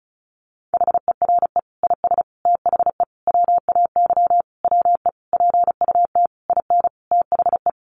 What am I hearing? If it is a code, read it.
Morse code, heard as a keyed tone, about 34 wpm, 700 Hz. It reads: HERE IS THE WAY WE PUT IN THE
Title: Sentences from Top 200 Words 35wpm